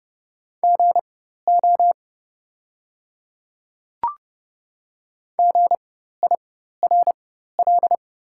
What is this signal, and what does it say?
Morse code, heard as a keyed tone, about 31 wpm, spaced as for 14 wpm, 700 Hz. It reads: GO GIRL
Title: 4x Character Spacing - Top 500 Words - 30wpm